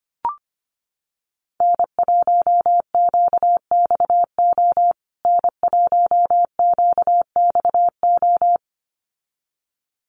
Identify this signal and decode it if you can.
Morse code, heard as a keyed tone, about 25 wpm, 700 Hz. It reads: N1QXO N1QXO
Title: USA Call Signs Repeated 2x 25wpm